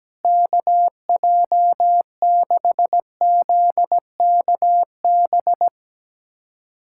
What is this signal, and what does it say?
Morse code, heard as a keyed tone, about 17 wpm, 700 Hz. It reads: KJ6ZKB
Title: USA Callsign DE Callsign 17wpm